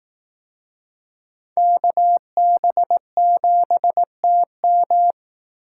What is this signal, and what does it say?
Morse code, heard as a keyed tone, about 19 wpm, 700 Hz. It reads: KB7TM
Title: Speed Racing - USA Call Signs 15wpm